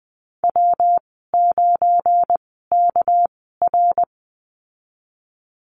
Morse code, heard as a keyed tone, about 20 wpm, spaced as for 14 wpm, 700 Hz. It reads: W9KR
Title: 245 Lesson 39g - Letter z - Call Signs 20wpm